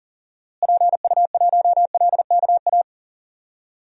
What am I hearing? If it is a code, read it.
Morse code, heard as a keyed tone, about 37 wpm, 700 Hz. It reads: PU1LKA
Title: DXCC-100 Call Signs - 6 Characters - Repeated 3x - 40wpm